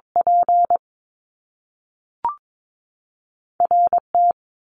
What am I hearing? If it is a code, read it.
Morse code, heard as a keyed tone, about 22 wpm, spaced as for 3 wpm, 700 Hz. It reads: PR T
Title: Top Two Letter Combinations in Words 22wpm